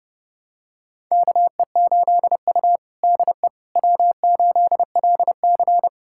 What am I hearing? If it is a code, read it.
Morse code, heard as a keyed tone, about 30 wpm, 700 Hz. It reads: KE8U DE W8LC
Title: USA Callsign DE Callsign - 4 Characters - 30wpm